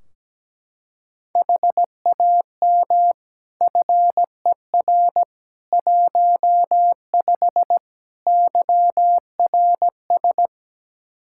Morse code, heard as a keyed tone, about 17 wpm, 700 Hz. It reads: HAM FER 15 YRS
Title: K1USN SST QSO Practice 17wpm